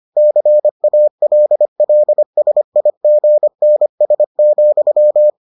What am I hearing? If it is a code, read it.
Morse code, heard as a keyed tone, about 25 wpm, 600 Hz. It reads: CALLSIGNS,